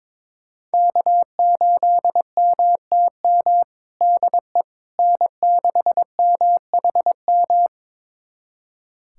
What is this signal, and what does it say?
Morse code, heard as a keyed tone, about 22 wpm, 700 Hz. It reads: K8MTM DE N6MHM
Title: USA Callsign DE Callsign - 5 Characters - 22wpm